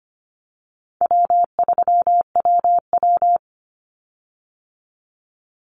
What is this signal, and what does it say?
Morse code, heard as a keyed tone, about 25 wpm, 700 Hz. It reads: W3WW
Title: Call Signs Repeated 2x - 25wpm